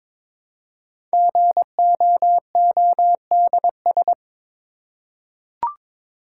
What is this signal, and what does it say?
Morse code, heard as a keyed tone, about 22 wpm, 700 Hz. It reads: GOODS
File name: 5 Letter Words - First 3 Letters Given - 22wpm